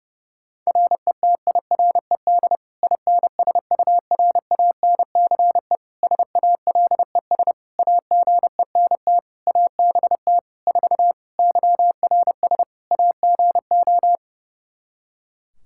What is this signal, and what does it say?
Morse code, heard as a keyed tone, about 30 wpm, 700 Hz. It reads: RETIRED INSURANCE SALES AGENT ABT 4 YRS AGO
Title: K1USN SST QSO Practice 30wpm